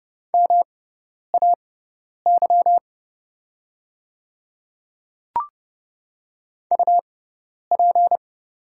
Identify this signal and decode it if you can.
Morse code, heard as a keyed tone, about 29 wpm, spaced as for 10 wpm, 690 Hz. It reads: MAY UP
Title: 6x Character Spacing - Top 100 Words - 30wpm